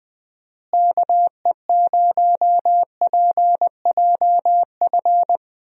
Morse code, heard as a keyed tone, about 20 wpm, 700 Hz. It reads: KE0PJF
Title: USA Call Signs - 6 Characters - Repeated 3x - 20wpm